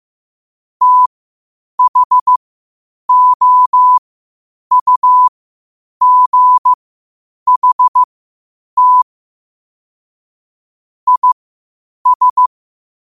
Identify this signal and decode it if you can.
Morse code, heard as a keyed tone, about 15 wpm, spaced as for 8 wpm, 1000 Hz. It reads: THOUGHT IS
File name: qotdfiles / 8wpm